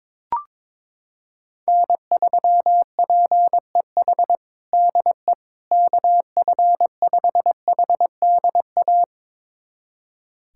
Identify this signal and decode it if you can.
Morse code, heard as a keyed tone, about 22 wpm, 700 Hz. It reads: N3PEH DE KF5HDA
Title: USA Callsign DE Callsign 22wpm